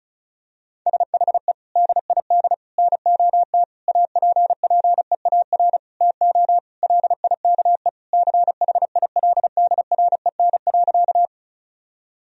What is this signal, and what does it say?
Morse code, heard as a keyed tone, about 36 wpm, 700 Hz. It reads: SHE DID NOT APPEAR TO LIKE CHILDREN.
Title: Sentences from Top 1000 Words - Review Part 2 - 35wpm